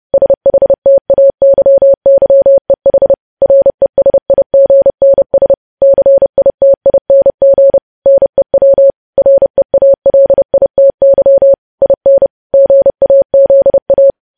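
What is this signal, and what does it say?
Morse code, heard as a keyed tone, about 30 wpm, 550 Hz. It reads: SHTAYYEH RESIGNS CITING NEW REALITY IN GAZA
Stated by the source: News headlines sent in Morse code at 30.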